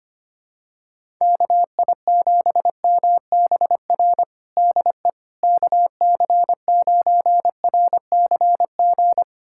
Morse code, heard as a keyed tone, about 25 wpm, 700 Hz. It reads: KI7MBR DE KC9RCG
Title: USA Callsign DE Callsign - 6 Characters - 25wpm